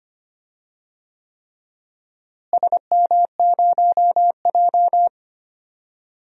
Morse code, heard as a keyed tone, about 25 wpm, 700 Hz. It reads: SM0J
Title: DXCC-100 Call Signs - 4 Characters - Repeated 3x - 25wpm